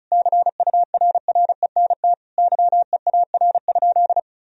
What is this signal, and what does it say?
Morse code, heard as a keyed tone, about 34 wpm, 700 Hz. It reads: CURRENT YEAR?
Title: Sentences from Top 1000 Words - Review Part 4 - 35wpm